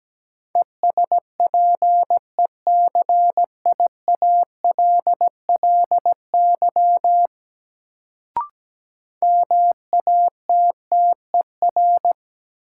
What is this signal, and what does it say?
Morse code, heard as a keyed tone, about 17 wpm, 700 Hz. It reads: ESPECIALLY MATTER
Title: Sentences from Top 700 Words - New Word Drill - 17wpm